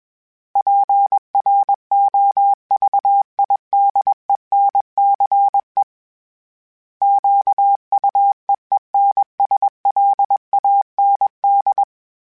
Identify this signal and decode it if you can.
Morse code, heard as a keyed tone, about 21 wpm, 800 Hz. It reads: PROVIDENCE QUEENSLAND
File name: fromSMs / FromSMs mp3s / Cities@20